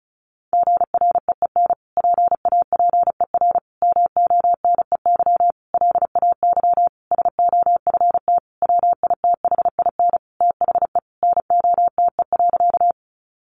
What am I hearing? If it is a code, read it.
Morse code, heard as a keyed tone, about 34 wpm, 700 Hz. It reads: GREEN PAPER MONEY LAY SOFT WITHIN THE NOTE.
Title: Sentences from Top 1500 Words 35wpm